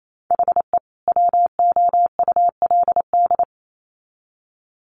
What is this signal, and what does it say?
Morse code, heard as a keyed tone, about 28 wpm, 700 Hz. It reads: HE WOULD
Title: Sets of 2 Words 28wpm